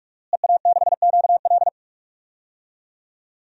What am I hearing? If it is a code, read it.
Morse code, heard as a keyed tone, about 46 wpm, 700 Hz. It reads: EA6QL
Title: DXCC-100 Call Signs - 5 Characters - Repeated 2x - 45wpm